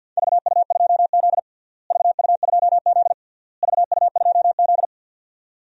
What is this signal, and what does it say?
Morse code, heard as a keyed tone, about 49 wpm, 700 Hz. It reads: VU2Z VU2Z VU2Z
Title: DXCC-100 Call Signs - 4 Characters - Repeated 3x - 50wpm